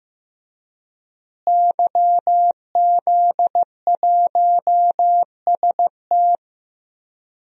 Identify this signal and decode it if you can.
Morse code, heard as a keyed tone, about 15 wpm, 700 Hz. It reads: YZ1ST
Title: DXCC-100 Call Signs - 5 Characters - Repeated 3x - 15wpm